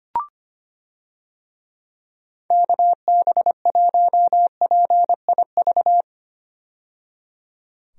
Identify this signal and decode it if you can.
Morse code, heard as a keyed tone, about 25 wpm, 700 Hz. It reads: KB1PIV
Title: USA Call Signs - 6 Characters - 25wpm